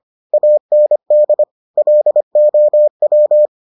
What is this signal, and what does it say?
Morse code, heard as a keyed tone, about 25 wpm, 600 Hz. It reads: AND LOW